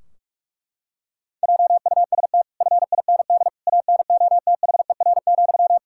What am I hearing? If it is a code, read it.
Morse code, heard as a keyed tone, about 43 wpm, 700 Hz. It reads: JUST FIND ANOTHER,
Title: Sentences from Top 1600 Words 45wpm